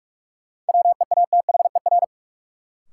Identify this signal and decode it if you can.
Morse code, heard as a keyed tone, about 46 wpm, 700 Hz. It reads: WEATHER
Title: Sentences from Top 800 Words - New Word Drill - 45wpm